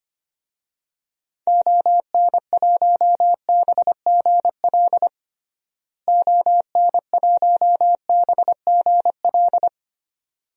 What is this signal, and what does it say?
Morse code, heard as a keyed tone, about 25 wpm, 700 Hz. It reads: ON1BGL ON1BGL
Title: DXCC-100 Call Signs - 6 Characters - Repeated 2x - 25wpm